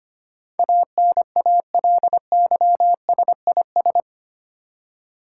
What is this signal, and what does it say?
Morse code, heard as a keyed tone, about 25 wpm, 700 Hz. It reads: ANALYSIS
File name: Eight Letter Words - 25wpm